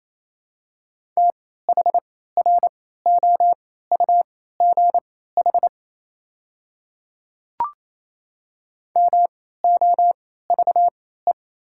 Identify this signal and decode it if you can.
Morse code, heard as a keyed tone, about 28 wpm, spaced as for 16 wpm, 700 Hz. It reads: THROUGH MOVE
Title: 3x Character Spacing - Top 300 Words - 28wpm